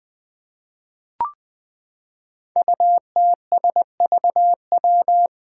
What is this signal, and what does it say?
Morse code, heard as a keyed tone, about 20 wpm, 700 Hz. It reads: UTSVW
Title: Five Challenging Letter-Number Combinations 20wpm